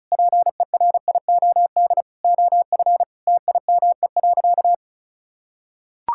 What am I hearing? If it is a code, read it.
Morse code, heard as a keyed tone, about 36 wpm, 700 Hz. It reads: PERIOD OF TIME.
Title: Sentences from Top 1000 Words - Review Part 5 - 35wpm